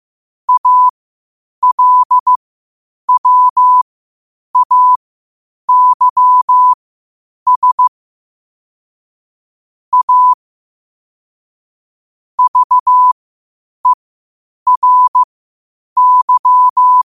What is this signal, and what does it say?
Morse code, heard as a keyed tone, about 15 wpm, spaced as for 8 wpm, 1000 Hz. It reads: ALWAYS A VERY
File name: qotdfiles / 8wpm